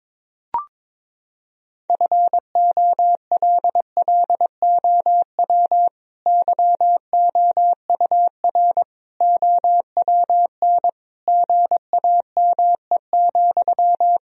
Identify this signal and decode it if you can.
Morse code, heard as a keyed tone, about 22 wpm, 700 Hz. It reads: FOLLOW YOUR OWN GAME,
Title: Sentences from Top 1000 Words - Review Part 3 - 22wpm